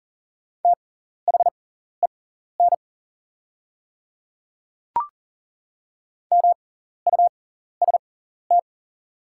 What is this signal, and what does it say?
Morse code, heard as a keyed tone, about 39 wpm, spaced as for 14 wpm, 700 Hz. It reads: THEN MUST
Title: 6x Character Spacing - Top 200 Words - 40wpm